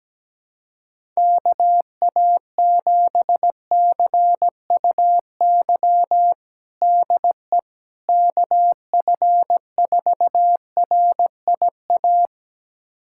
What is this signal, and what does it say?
Morse code, heard as a keyed tone, about 17 wpm, 700 Hz. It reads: KA7CUY DE KF4RIA
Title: USA Callsign DE Callsign 17wpm